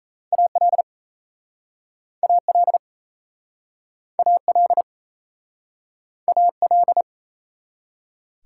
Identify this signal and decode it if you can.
Morse code, heard as a keyed tone, about 38 wpm, 700 Hz. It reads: AL AL AL AL